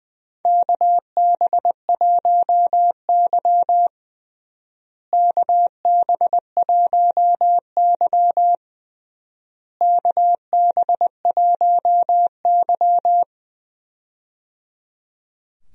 Morse code, heard as a keyed tone, about 21 wpm, 700 Hz. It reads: KB1Y KB1Y KB1Y
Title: USA Call Signs - 4 Characters - Repeated 3x - 20wpm